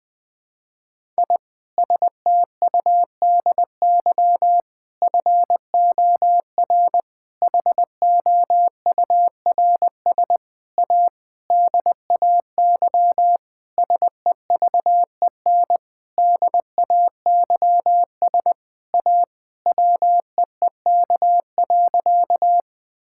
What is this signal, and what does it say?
Morse code, heard as a keyed tone, about 21 wpm, 700 Hz. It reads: I STUDY FOR HOURS A DAY SEVEN DAYS A WEEK.
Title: Sentences with 7-Word Phrases - 20wpm